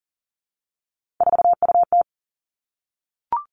Morse code, heard as a keyed tone, about 39 wpm, 700 Hz. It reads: 4UT
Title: Three Challenging Letter-Number Combinations 40wpm